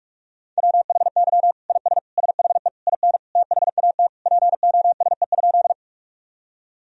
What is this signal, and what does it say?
Morse code, heard as a keyed tone, about 46 wpm, 700 Hz. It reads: WHY IS SHE IN THAT POSE?
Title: Sentences from Top 1000 Words - Review Part 2 - 45wpm